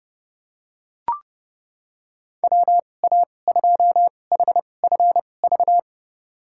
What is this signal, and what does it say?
Morse code, heard as a keyed tone, about 32 wpm, spaced as for 22 wpm, 700 Hz. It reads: WA2HFV
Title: USA Call Signs - 6 Characters - 2x Character Spacing - 30wpm